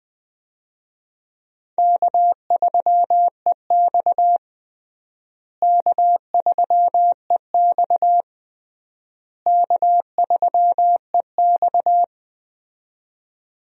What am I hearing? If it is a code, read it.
Morse code, heard as a keyed tone, about 20 wpm, 700 Hz. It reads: K3EX K3EX K3EX
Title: USA Call Signs - 4 Characters - Repeated 3x - 20wpm